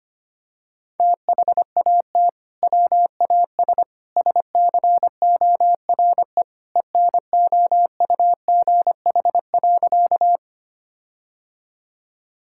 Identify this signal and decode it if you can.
Morse code, heard as a keyed tone, about 25 wpm, 700 Hz. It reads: THAT WAS SCORE ENOUGH.
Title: Sentences from Top 1600 Words 25wpm